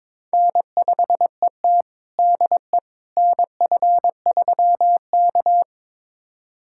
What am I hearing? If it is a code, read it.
Morse code, heard as a keyed tone, about 22 wpm, 700 Hz. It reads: N5ET DE NF3K
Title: USA Callsign DE Callsign - 4 Characters - 22wpm